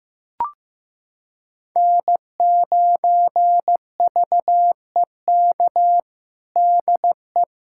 Morse code, heard as a keyed tone, about 15 wpm, 700 Hz. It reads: N9VEK DE
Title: USA Callsign DE Callsign 15wpm